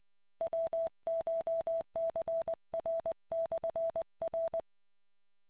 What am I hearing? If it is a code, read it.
Morse code, heard as a keyed tone, about 20 wpm, 660 Hz. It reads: WCHCR/R